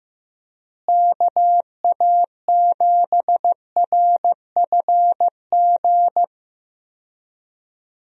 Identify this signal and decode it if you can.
Morse code, heard as a keyed tone, about 15 wpm, 700 Hz. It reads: KA7RFG
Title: USA Call Signs - 6 Characters - Repeated 2x - 15wpm